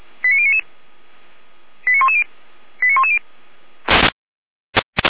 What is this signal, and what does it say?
Five-tone selective call (ZVEI), heard as quick runs of five tones, tones of about 70 ms. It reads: 89099 89110 89110
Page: /455/SDR/ZVEI/89097/